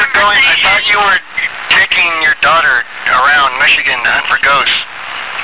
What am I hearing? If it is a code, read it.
Emergency Alert System (SAME) tones, heard as harsh buzzing bursts, with speech. It is heard from 1.1 to 4.4 s.